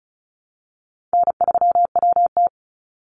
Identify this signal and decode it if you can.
Morse code, heard as a keyed tone, about 34 wpm, 700 Hz. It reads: N3WT
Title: Easy Call Signs Repeated 2x - 35wpm